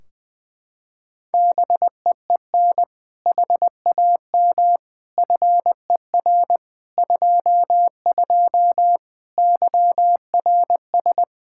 Morse code, heard as a keyed tone, about 20 wpm, 700 Hz. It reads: BEEN HAM FER 22 YRS
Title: Practice QSO - Exchange 3 - More Info - 20wpm